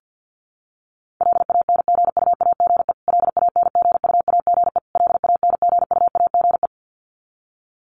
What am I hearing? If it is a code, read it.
Morse code, heard as a keyed tone, about 50 wpm, 700 Hz. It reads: LANGUAGE LANGUAGE LANGUAGE
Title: Top 500 Words Repeated 50wpm